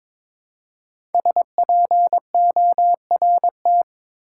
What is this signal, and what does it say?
Morse code, heard as a keyed tone, about 22 wpm, 700 Hz. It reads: SPORT
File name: 5 Letter Words - First 3 Letters Given - 22wpm